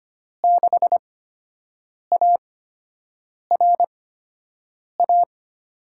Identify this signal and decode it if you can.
Morse code, heard as a keyed tone, about 25 wpm, spaced as for 7 wpm, 700 Hz. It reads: 6ARA
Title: USA Call Signs - 6 Characters - 8x Character Spacing - 25wpm